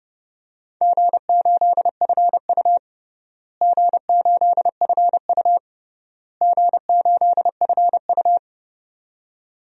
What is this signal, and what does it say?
Morse code, heard as a keyed tone, about 30 wpm, 700 Hz. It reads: G8FU G8FU G8FU
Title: DXCC-100 Call Signs - 4 Characters - Repeated 3x - 30wpm